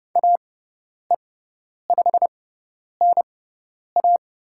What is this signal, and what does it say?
Morse code, heard as a keyed tone, about 30 wpm, spaced as for 10 wpm, 700 Hz. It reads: AE5NA
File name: Lesson 18g - Letter m - Call Signs 30wpm